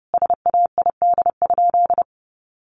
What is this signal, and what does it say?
Morse code, heard as a keyed tone, about 30 wpm, 700 Hz. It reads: SAID?
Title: Sentences from Top 100 Words 30wpm